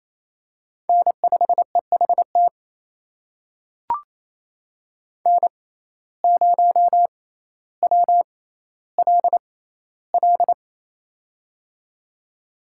Morse code, heard as a keyed tone, about 28 wpm, spaced as for 14 wpm, 700 Hz. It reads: N5EHT N0WLL
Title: USA Call Signs - 5 Characters - 6x Character Spacing - 28wpm